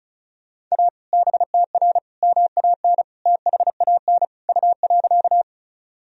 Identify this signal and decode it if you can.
Morse code, heard as a keyed tone, about 34 wpm, 700 Hz. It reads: A BTR MAN THAN U.
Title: Sentences from Top 400 Words and Abbreviations 35wpm